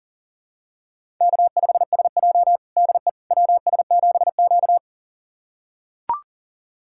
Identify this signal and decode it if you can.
Morse code, heard as a keyed tone, about 40 wpm, 700 Hz. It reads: K5SJ DE WS7Q
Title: USA Callsign DE Callsign - 4 Characters - 40wpm